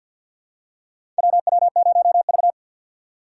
Morse code, heard as a keyed tone, about 51 wpm, 700 Hz. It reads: WW0V
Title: USA Call Signs - 2x Character Spacing - 50wpm